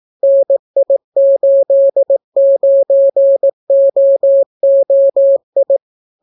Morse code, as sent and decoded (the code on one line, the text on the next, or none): -. .. ---.. ----. --- --- ..
NI89OOI